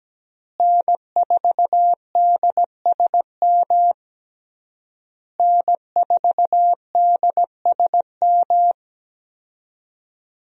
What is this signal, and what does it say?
Morse code, heard as a keyed tone, about 17 wpm, 700 Hz. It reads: N4DSM N4DSM
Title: USA Call Signs - 5 Characters - Repeated 3x - 17wpm